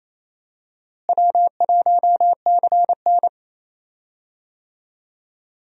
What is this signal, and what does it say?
Morse code, heard as a keyed tone, about 28 wpm, 700 Hz. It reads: W1CN
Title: USA Call Signs - 4 Characters - Repeated 3x - 28wpm